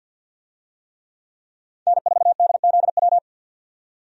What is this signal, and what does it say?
Morse code, heard as a keyed tone, about 47 wpm, 700 Hz. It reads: N4DZW
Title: USA Call Signs - 5 Characters - 50wpm